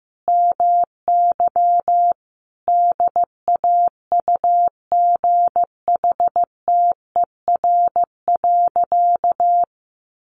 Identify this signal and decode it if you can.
Morse code, heard as a keyed tone, about 15 wpm, 700 Hz. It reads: MY DAUGHTER.
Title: Sentences from Top 1500 Words 15wpm